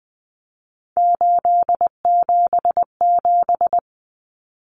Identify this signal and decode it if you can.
Morse code, heard as a keyed tone, about 20 wpm, 700 Hz. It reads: 877
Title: Common Cut Numbers - 3-Digit 20wpm